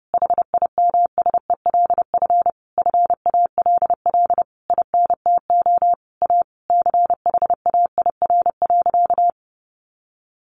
Morse code, heard as a keyed tone, about 30 wpm, 700 Hz. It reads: HIMSELF FALL INTO A CHAIR.
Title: A Sound Of Thunder 30wpm